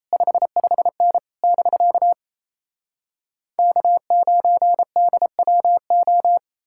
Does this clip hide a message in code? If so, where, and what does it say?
Yes: Morse code from 0.1 to 6.4 s, heard as a keyed tone, about 37 wpm, 700 Hz. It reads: 55N <BK> K9DWO